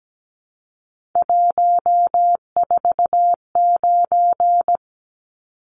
Morse code, heard as a keyed tone, about 17 wpm, 700 Hz. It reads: 149
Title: Numbers - 3-Digit 17wpm